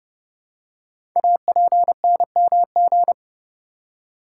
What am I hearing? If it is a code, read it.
Morse code, heard as a keyed tone, about 30 wpm, 700 Hz. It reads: APNMG
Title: Code Groups - 5 Letters - 30wpm